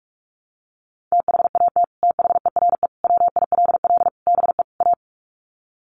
Morse code, heard as a keyed tone, about 44 wpm, 700 Hz. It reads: THAT THERE WILL BE A
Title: Sets of 5 Words Encore2 45wpm